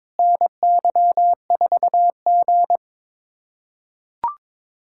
Morse code, heard as a keyed tone, about 22 wpm, 700 Hz. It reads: NY4G
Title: K1USN SST QSO Practice - 22wpm